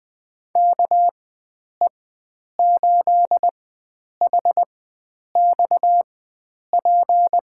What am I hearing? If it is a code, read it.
Morse code, heard as a keyed tone, about 20 wpm, spaced as for 9 wpm, 700 Hz. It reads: KE8HXP
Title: USA Call Signs - 6 Characters - 4x Character Spacing - 20wpm